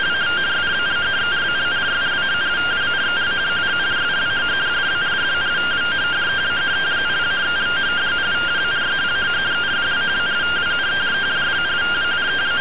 Two-tone teleprinter signal, about 170 Hz shift.
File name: findings / MR AXIAL 71mm 3mm.gen-13 rtty rx